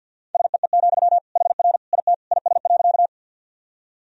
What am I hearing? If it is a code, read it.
Morse code, heard as a keyed tone, about 52 wpm, 700 Hz. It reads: SEE, HR IT IS.
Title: Sentences from Top 200 Words and Abbreviations 50wpm